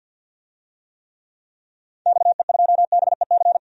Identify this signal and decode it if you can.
Morse code, heard as a keyed tone, about 53 wpm, 700 Hz. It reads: XE2BEC
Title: DXCC-100 Call Signs - 6 Characters - 50wpm